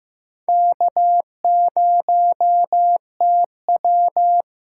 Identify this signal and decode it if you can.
Morse code, heard as a keyed tone, about 15 wpm, 700 Hz. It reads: K0TW
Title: USA Call Signs - 8x Character Spacing - 15wpm